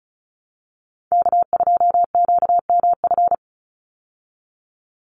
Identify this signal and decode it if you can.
Morse code, heard as a keyed tone, about 34 wpm, 700 Hz. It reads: K2QMF
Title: Call Signs Repeated 2x - 35wpm